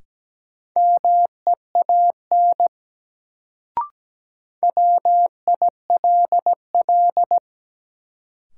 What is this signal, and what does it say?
Morse code, heard as a keyed tone, about 17 wpm, 700 Hz. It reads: MEAN WILL
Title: Sets of 2 Words - New Word Drill - 17wpm